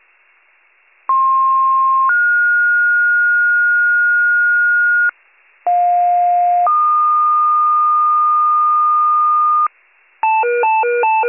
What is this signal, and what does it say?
Two-tone page, heard as a one-second tone, then a three-second tone, about 1060 then 1430 Hz, about 710 then 1220 Hz, with noise. It is heard from 1.1 to 9.7 s.